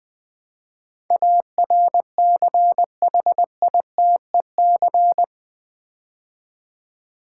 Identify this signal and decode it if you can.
Morse code, heard as a keyed tone, about 20 wpm, 700 Hz. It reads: ARCHITEC
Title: Build A Word - 4 Letters At A Time - 12-Letter Words - 20wpm